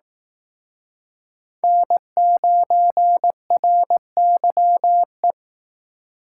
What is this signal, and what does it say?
Morse code, heard as a keyed tone, about 18 wpm, 700 Hz. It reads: N9RYE
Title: Speed Racing - USA 1x3 Call Signs 15wpm